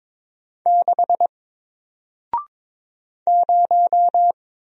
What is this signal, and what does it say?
Morse code, heard as a keyed tone, about 22 wpm, 700 Hz. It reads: 6 0
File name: Single-Letter-Number-22wpm